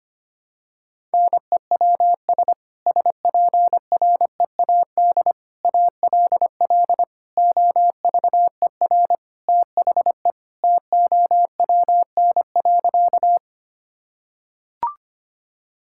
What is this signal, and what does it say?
Morse code, heard as a keyed tone, about 25 wpm, 700 Hz. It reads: NEWS SPREAD ALL OVER THE TOWN.
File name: practice / 25wpm / Sentences with 3-Word Phrases - 25wpm